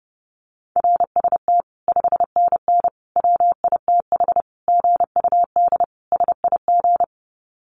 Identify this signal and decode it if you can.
Morse code, heard as a keyed tone, about 30 wpm, 700 Hz. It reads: RST 5NN WITH GUD SIG
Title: Common QSO Phrases 30wpm